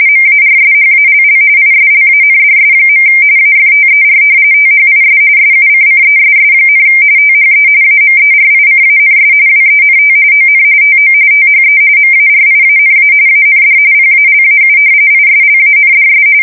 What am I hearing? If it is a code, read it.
RTTY, about 45.45 baud. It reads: MARSHALL ISLANDS. AFTER THIS SUCCESS, THE AMERICANS WENT ON TO THE MARIANA AND PALAU ISLANDS
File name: fsk170 / usnavy4